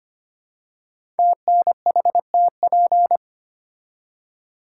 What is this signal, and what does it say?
Morse code, heard as a keyed tone, about 25 wpm, 700 Hz. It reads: TNHTP
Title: Code Groups - 5 Letters - 25wpm